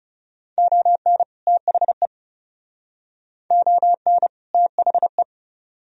Morse code, heard as a keyed tone, about 38 wpm, 700 Hz. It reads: ON THE ON THE